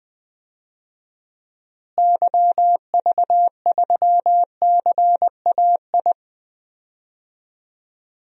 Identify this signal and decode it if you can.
Morse code, heard as a keyed tone, about 20 wpm, 700 Hz. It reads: YV3CAI
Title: DXCC-100 Call Signs - 6 Characters - Repeated 3x - 20wpm